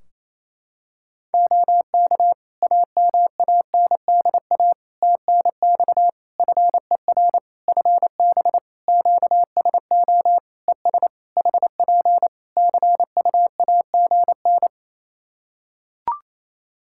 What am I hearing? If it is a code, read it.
Morse code, heard as a keyed tone, about 28 wpm, 700 Hz. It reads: OK AMANDA TNX FER FB QSO ES HP CUAGN
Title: K1USN SST QSO Practice 28wpm